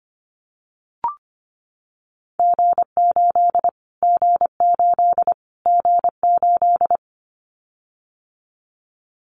Morse code, heard as a keyed tone, about 25 wpm, 700 Hz. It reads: G8 G8 G8
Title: Callsign Prefixes for Top 25 DXCC - Repeated 3x - 25wpm